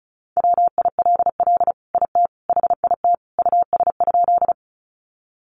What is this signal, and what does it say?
Morse code, heard as a keyed tone, about 34 wpm, 700 Hz. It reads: WILL IT HIT US?
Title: Sentences from Top 1200 Words 35wpm